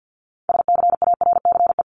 Morse code, heard as a keyed tone, about 50 wpm, 700 Hz. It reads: SCARCE
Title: LY Suffix - Top 100 Words with Base Words 50wpm